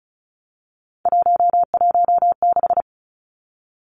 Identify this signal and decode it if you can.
Morse code, heard as a keyed tone, about 34 wpm, 700 Hz. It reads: 116
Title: Three Challenging Letter-Number Combinations 35wpm